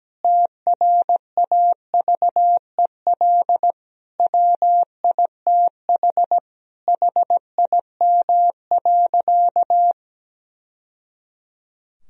Morse code, heard as a keyed tone, about 17 wpm, 700 Hz. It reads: TRAVEL WITH HIM.
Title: Sentences from Top 1000 Words - Review Part 2 - 17wpm